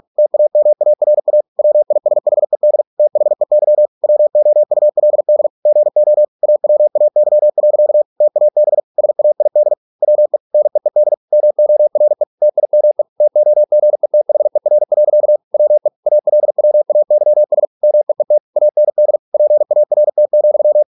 Morse code, heard as a keyed tone, about 46 wpm, 600 Hz. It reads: TAMARA WISHED THEY WOULD GO AWAY. TAB SAID WE NEED MORE TIME TOGETHER. WE ALWAYS MEET AND PART,